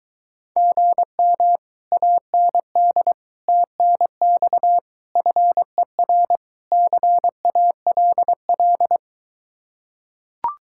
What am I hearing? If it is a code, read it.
Morse code, heard as a keyed tone, about 24 wpm, 700 Hz. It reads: GM AND TNX FER CALL